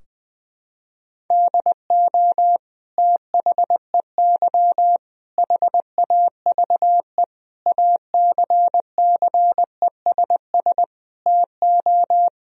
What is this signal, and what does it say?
Morse code, heard as a keyed tone, about 21 wpm, 700 Hz. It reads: DO THEY HAVE ACCESS TO
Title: Sentences from Top 1000 Words - Review Part 4 - 20wpm